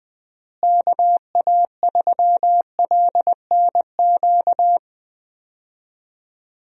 Morse code, heard as a keyed tone, about 20 wpm, 700 Hz. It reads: KA3LNQ
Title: USA Call Signs - 6 Characters - Repeated 2x - 20wpm